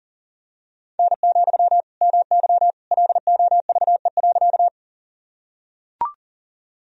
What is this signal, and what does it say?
Morse code, heard as a keyed tone, about 40 wpm, 700 Hz. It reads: N, MY LOVE.
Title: Sentences from Top 400 Words and Abbreviations 40wpm